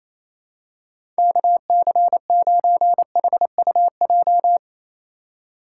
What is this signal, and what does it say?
Morse code, heard as a keyed tone, about 28 wpm, 700 Hz. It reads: KC9HUJ
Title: USA Call Signs - 4x Character Spacing - 28wpm